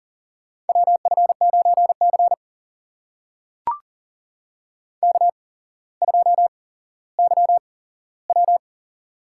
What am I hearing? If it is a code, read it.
Morse code, heard as a keyed tone, about 40 wpm, spaced as for 17 wpm, 700 Hz. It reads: WF9C K2YW
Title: USA Call Signs - 4 Characters - 8x Character Spacing - 40wpm